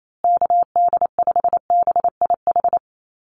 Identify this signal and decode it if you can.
Morse code, heard as a keyed tone, about 28 wpm, 700 Hz. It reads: KD5BIH
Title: USA Call Signs Repeated 3x 28wpm